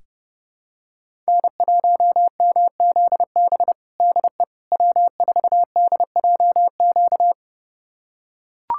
Morse code, heard as a keyed tone, about 32 wpm, 700 Hz. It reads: N1MZB DE W4DJQ
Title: USA Callsign DE Callsign - 5 Characters - 30wpm